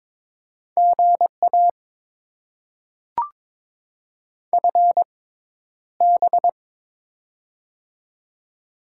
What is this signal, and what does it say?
Morse code, heard as a keyed tone, about 22 wpm, spaced as for 11 wpm, 690 Hz. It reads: GA FB
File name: Top 20 QSO Elements - 6x Character Spacing - 22wpm